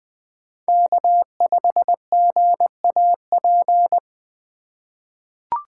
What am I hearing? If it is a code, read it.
Morse code, heard as a keyed tone, about 20 wpm, 700 Hz. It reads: K5GAP
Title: USA Call Signs - 5 Characters - 2x Character Spacing - 20wpm